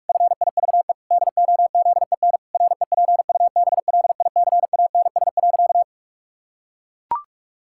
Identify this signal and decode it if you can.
Morse code, heard as a keyed tone, about 44 wpm, 700 Hz. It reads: FIVE DOZEN REPUBLICANS.
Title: Pangram Challenge v2 - 45wpm